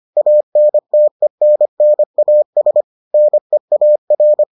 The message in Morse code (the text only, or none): ANTENNAS NEAR